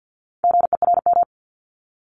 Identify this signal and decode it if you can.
Morse code, heard as a keyed tone, about 50 wpm, 700 Hz. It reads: GERM
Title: Multi-letter Phonemes 50wpm